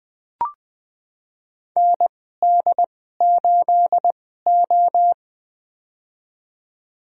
Morse code, heard as a keyed tone, about 20 wpm, spaced as for 14 wpm, 700 Hz. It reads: ND8O
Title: USA Call Signs - 2x Character Spacing - 20wpm